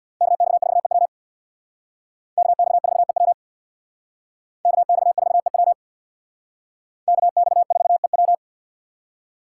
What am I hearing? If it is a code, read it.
Morse code, heard as a keyed tone, about 62 wpm, 700 Hz. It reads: KX4EW KX4EW KX4EW KX4EW
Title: Speed Racing - USA 2x2 Call Signs 50wpm